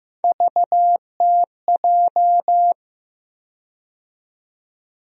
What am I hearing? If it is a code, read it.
Morse code, heard as a keyed tone, about 15 wpm, 700 Hz. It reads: VTJ
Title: USA Call Sign Postfixes - 3 Letters 15wpm